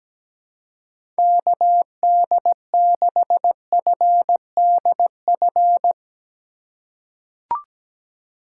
Morse code, heard as a keyed tone, about 17 wpm, 700 Hz. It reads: KD6FDF
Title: USA Call Signs - 6 Characters - 4x Character Spacing - 17wpm